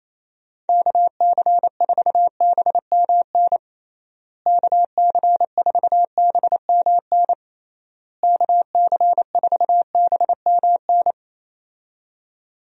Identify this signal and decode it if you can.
Morse code, heard as a keyed tone, about 28 wpm, 700 Hz. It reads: KC4BMN KC4BMN KC4BMN
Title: USA Call Signs - 6 Characters - Repeated 3x - 28wpm